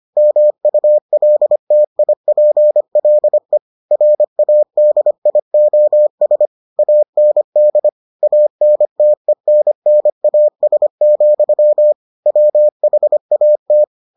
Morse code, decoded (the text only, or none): MULTIPLE RADIOS AND ANTENNAS, WHAT